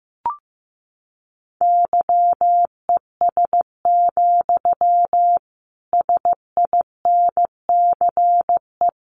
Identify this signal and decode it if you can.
Morse code, heard as a keyed tone, about 15 wpm, 700 Hz. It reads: YES, SINCE
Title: Sentences from Top 1300 Words 15wpm